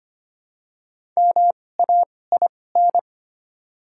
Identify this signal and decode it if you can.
Morse code, heard as a keyed tone, about 25 wpm, spaced as for 18 wpm, 700 Hz. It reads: MAIN
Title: 2x Character Spacing - Top 500 Words - 25wpm